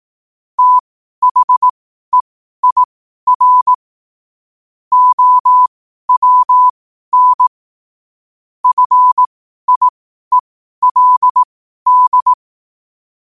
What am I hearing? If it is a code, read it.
Morse code, heard as a keyed tone, about 18 wpm, spaced as for 12 wpm, 1000 Hz. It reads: THEIR OWN FIELD
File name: qotdfiles / 12wpm